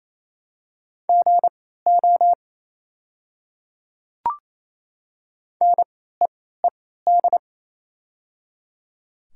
Morse code, heard as a keyed tone, about 27 wpm, spaced as for 16 wpm, 700 Hz. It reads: GO NEED
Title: 3x Character Spacing - Top 200 Words - 28wpm